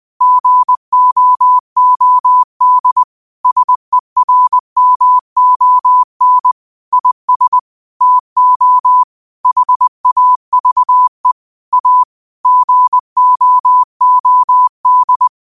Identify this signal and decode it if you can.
Morse code, heard as a keyed tone, about 20 wpm, 1000 Hz. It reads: GOOD SERMON IS TO HAVE A GOOD